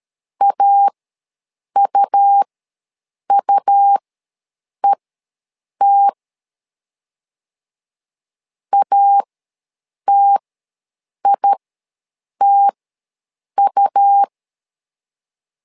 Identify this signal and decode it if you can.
Morse code, heard as a keyed tone, about 13 wpm, spaced as for 7 wpm, 800 Hz. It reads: AUUET ATITU
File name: LEZIONI / 2 UHDN / 2 U ws 12wpm 1655